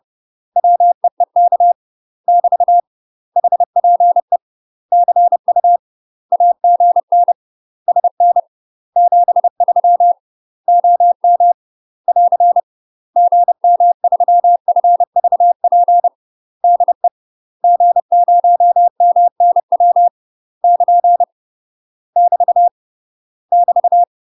Morse code, heard as a keyed tone, about 30 wpm, 700 Hz. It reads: WEEK <BT> HPE CU AGN SN 73 OM <AR> GM3FVP DE G0MNW <KN> <BT> <BT>
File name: UK MASTER QSOs TRAINING / 30WPM / UK MASTER QSOs 0015